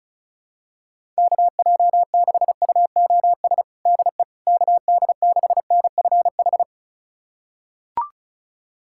Morse code, heard as a keyed tone, about 36 wpm, 700 Hz. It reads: KJ6UOS DE KD6NFH
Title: USA Callsign DE Callsign - 6 Characters - 35wpm